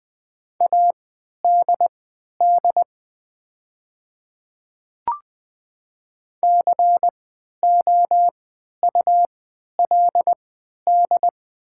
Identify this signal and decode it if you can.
Morse code, heard as a keyed tone, about 20 wpm, spaced as for 11 wpm, 700 Hz. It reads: ADD COULD
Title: 3x Character Spacing - Top 200 Words - 20wpm